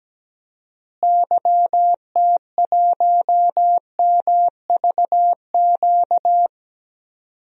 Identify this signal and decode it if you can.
Morse code, heard as a keyed tone, about 17 wpm, 700 Hz. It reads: YT1MVQ
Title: DXCC-100 Call Signs - 6 Characters - Repeated 3x - 17wpm